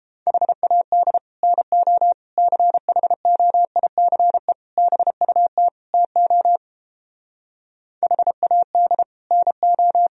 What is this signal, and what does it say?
Morse code, heard as a keyed tone, about 29 wpm, 700 Hz. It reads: HAD NO CHOICE BUT TO HAD NO